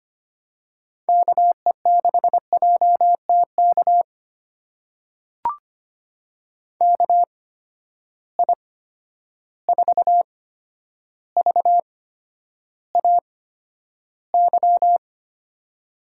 Morse code, heard as a keyed tone, about 25 wpm, spaced as for 11 wpm, 700 Hz. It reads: KE6JTK KI4VAY
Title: USA Call Signs - 6 Characters - 8x Character Spacing - 25wpm